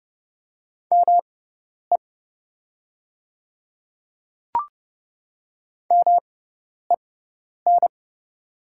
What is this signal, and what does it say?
Morse code, heard as a keyed tone, about 29 wpm, spaced as for 10 wpm, 690 Hz. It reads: ME MEN
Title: 6x Character Spacing - Top 500 Words - 30wpm